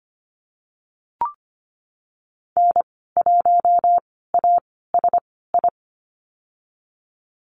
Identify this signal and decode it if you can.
Morse code, heard as a keyed tone, about 25 wpm, spaced as for 16 wpm, 700 Hz. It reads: N1ASI
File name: Lesson 15g - Number 5 - Call Signs 25wpm